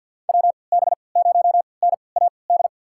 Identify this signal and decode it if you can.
Morse code, heard as a keyed tone, about 45 wpm, spaced as for 27 wpm, 700 Hz. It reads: WB0NAD
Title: USA Call Signs - 6 Characters - 3x Character Spacing - 50wpm